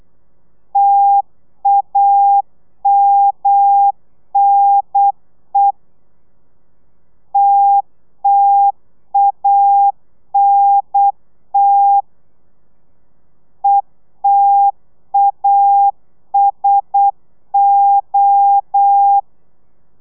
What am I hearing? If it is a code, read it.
Morse code, heard as a keyed tone, about 8 wpm, 800 Hz. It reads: TAMNE TTANT ETASO